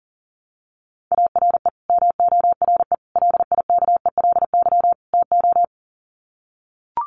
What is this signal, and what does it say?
Morse code, heard as a keyed tone, about 40 wpm, 700 Hz. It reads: ARE MORE LIKELY TO
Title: Sets of 4 Words 40wpm